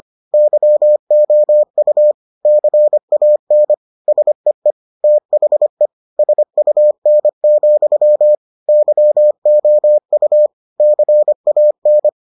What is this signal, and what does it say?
Morse code, heard as a keyed tone, about 25 wpm, 600 Hz. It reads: YOU CAN SEE THE SUN, YOU CAN